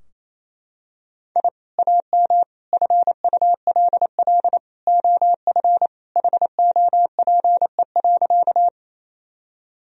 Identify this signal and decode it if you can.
Morse code, heard as a keyed tone, about 27 wpm, 700 Hz. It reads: I AM FULL OF HOPE.
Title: Sentences from Top 1000 Words - Review Part 3 - 28wpm